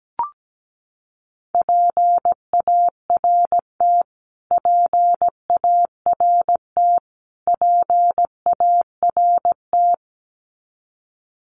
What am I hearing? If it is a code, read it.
Morse code, heard as a keyed tone, about 17 wpm, 700 Hz. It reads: PART PART PART
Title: Top 300 Words Repeated 17wpm